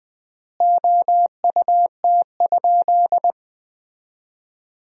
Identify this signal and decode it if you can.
Morse code, heard as a keyed tone, about 20 wpm, 700 Hz. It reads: OUT?
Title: 4x Word-Spacing - Sentences from Top 100 Words - 20wpm